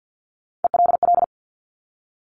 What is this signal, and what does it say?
Morse code, heard as a keyed tone, about 50 wpm, 700 Hz. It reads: ELL
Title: Top Three Letter Combinations in Words 50wpm